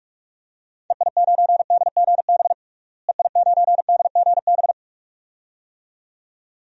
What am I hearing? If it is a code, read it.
Morse code, heard as a keyed tone, about 45 wpm, 700 Hz. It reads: EI9DGB EI9DGB
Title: DXCC-100 Call Signs - 6 Characters - Repeated 2x - 45wpm